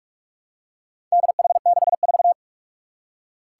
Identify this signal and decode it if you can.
Morse code, heard as a keyed tone, about 41 wpm, 700 Hz. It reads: DH64
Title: Four Challenging Letter-Number Combinations 45wpm